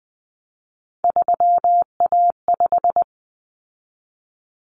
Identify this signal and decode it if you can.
Morse code, heard as a keyed tone, about 20 wpm, 700 Hz. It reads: 3A5
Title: Common Cut Numbers - 3-Digit 20wpm